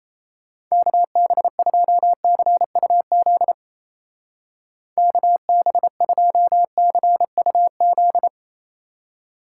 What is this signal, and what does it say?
Morse code, heard as a keyed tone, about 35 wpm, 700 Hz. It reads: KB2CUZ KB2CUZ
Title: Speed Racing - USA 2x3 Call Signs 28wpm